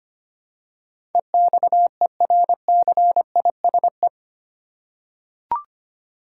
Morse code, heard as a keyed tone, about 25 wpm, 700 Hz. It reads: EXERCISE
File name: Eight Letter Words - 25wpm